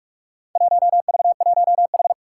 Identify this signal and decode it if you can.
Morse code, heard as a keyed tone, about 41 wpm, 700 Hz. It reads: 1V1H
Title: Four Challenging Letter-Number Combinations 45wpm